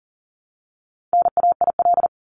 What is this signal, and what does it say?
Morse code, heard as a keyed tone, about 40 wpm, 700 Hz. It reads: NAIL
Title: Top 1501-1750 Words 40wpm